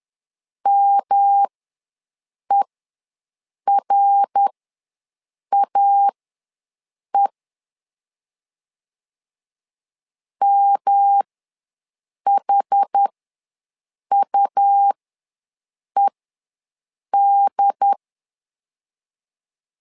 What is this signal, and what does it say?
Morse code, heard as a keyed tone, about 11 wpm, spaced as for 6 wpm, 800 Hz. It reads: MERAE MHUED
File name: LEZIONI / 3 MWRO / 3 MWR ws 10wpm 2151